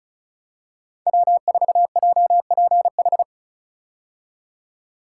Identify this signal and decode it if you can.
Morse code, heard as a keyed tone, about 36 wpm, 700 Hz. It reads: W4JPH
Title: K1USN SST QSO Practice - 35wpm